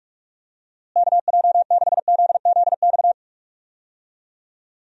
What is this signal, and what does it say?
Morse code, heard as a keyed tone, about 46 wpm, 700 Hz. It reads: KJ6ZZX
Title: USA Call Signs - 6 Characters - Repeated 2x - 45wpm